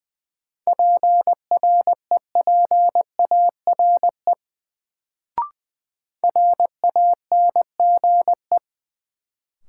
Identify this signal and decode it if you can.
Morse code, heard as a keyed tone, about 20 wpm, 700 Hz. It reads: PREPARE RANGE
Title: Sentences from Top 900 Words - New Word Drill - 20wpm